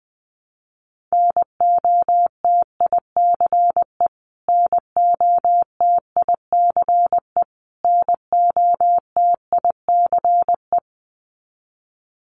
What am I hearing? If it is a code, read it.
Morse code, heard as a keyed tone, about 20 wpm, 700 Hz. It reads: NOTICE NOTICE NOTICE
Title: Top 500 Words Repeated 20wpm